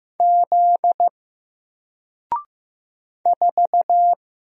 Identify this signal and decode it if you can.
Morse code, heard as a keyed tone, about 15 wpm, 700 Hz. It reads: Z 4
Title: Single Character (Letter:Number:Punctuation:Prosign) 15wpm